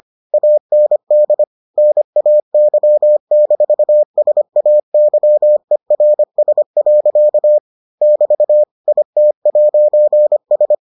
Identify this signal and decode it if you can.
Morse code, heard as a keyed tone, about 25 wpm, 600 Hz. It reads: AND NAY-SAYERS. <BT> IT'S